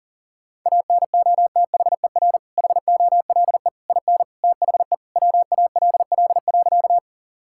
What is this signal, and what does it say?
Morse code, heard as a keyed tone, about 40 wpm, 700 Hz. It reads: ANOTHER HOLE IN THE WALL.
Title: Sentences from Top 1600 Words 40wpm